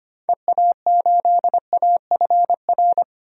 Morse code, heard as a keyed tone, about 25 wpm, 700 Hz. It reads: EA8AFR
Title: DXCC-100 Call Signs - 6 Characters - 25wpm